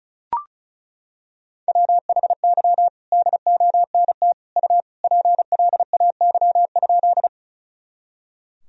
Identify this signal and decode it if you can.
Morse code, heard as a keyed tone, about 36 wpm, 700 Hz. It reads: WHY DONT U PLAY?
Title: Sentences from Top 300 Words and Abbreviations 35wpm